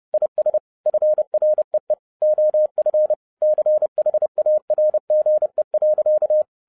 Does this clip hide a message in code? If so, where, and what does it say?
Yes: Morse code from 0.1 to 6.4 s, heard as a keyed tone, about 30 wpm, 610 Hz. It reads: IS FREE OF CHARGE.